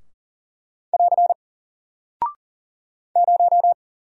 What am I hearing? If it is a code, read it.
Morse code, heard as a keyed tone, about 39 wpm, 700 Hz. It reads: <AR> 0
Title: Single Character (Letter:Number:Punctuation:Prosign) 40wpm